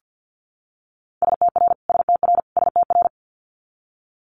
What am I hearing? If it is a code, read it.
Morse code, heard as a keyed tone, about 50 wpm, 700 Hz. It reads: STR STR STR
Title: Multi-letter Phonemes Repeated 3x 50wpm